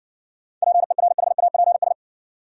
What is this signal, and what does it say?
Morse code, heard as a keyed tone, about 63 wpm, 700 Hz. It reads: PERHAPS